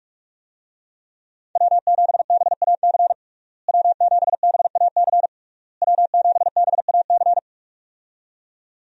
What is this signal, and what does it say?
Morse code, heard as a keyed tone, about 46 wpm, 700 Hz. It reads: W7BAC W7BAC W7BAC
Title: USA Call Signs - 5 Characters - Repeated 3x - 45wpm